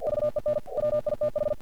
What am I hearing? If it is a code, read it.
Morse code, heard as a keyed tone, about 48 wpm, 600 Hz. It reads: VEN WITH
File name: archived / CRSnet